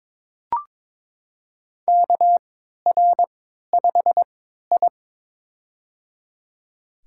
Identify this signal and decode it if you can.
Morse code, heard as a keyed tone, about 21 wpm, spaced as for 12 wpm, 700 Hz. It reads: KR5I
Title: USA Call Signs - 4 Characters - 3x Character Spacing - 22wpm